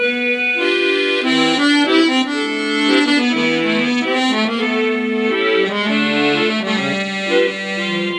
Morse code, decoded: none